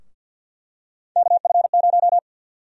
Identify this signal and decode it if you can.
Morse code, heard as a keyed tone, about 50 wpm, 700 Hz. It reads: KF0
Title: USA Call Sign Prefixes - Repeated 2x - 50wpm